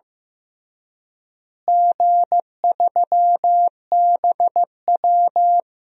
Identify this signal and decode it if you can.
Morse code, heard as a keyed tone, about 15 wpm, 700 Hz. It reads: G3BW
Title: DXCC-100 Call Signs - 4 Characters - Repeated 3x - 15wpm